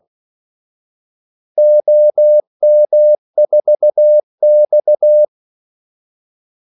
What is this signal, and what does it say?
Morse code, heard as a keyed tone, about 16 wpm, 600 Hz. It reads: OM4X
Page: Kaldesignaler 16 wpm